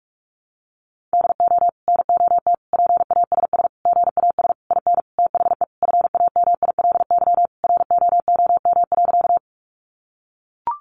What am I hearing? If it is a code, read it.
Morse code, heard as a keyed tone, about 44 wpm, 700 Hz. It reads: DO NOT PASS GAS IN THE FAMILY ROOM.
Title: Sentences from Top 1100 Words 45wpm